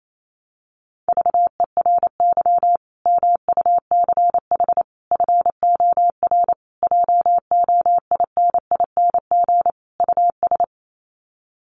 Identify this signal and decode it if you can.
Morse code, heard as a keyed tone, about 28 wpm, 700 Hz. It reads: VERY MUCH FOR JOINING US
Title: Sets of 5 Words 28wpm